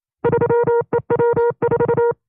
Morse code, heard as a keyed tone, about 27 wpm, 470 Hz. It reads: 3EW4
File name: FreeFiles / CW-practice / 25WPM-side1